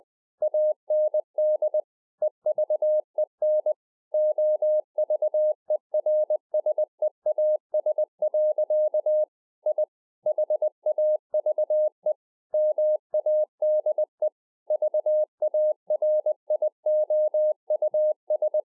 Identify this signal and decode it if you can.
Morse code, heard as a keyed tone, about 20 wpm, 610 Hz. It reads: AND EVEN OVERSEAS. I HAVE MADE VARIOUS
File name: 2011-04-mp3 / A SMALL TRANCEIVER 3 20WPM